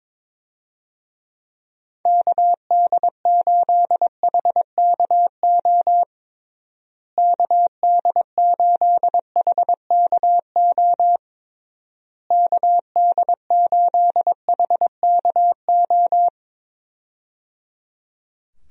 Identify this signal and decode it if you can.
Morse code, heard as a keyed tone, about 22 wpm, 700 Hz. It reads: KD8HKO KD8HKO KD8HKO
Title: USA Call Signs - 6 Characters - Repeated 3x - 22wpm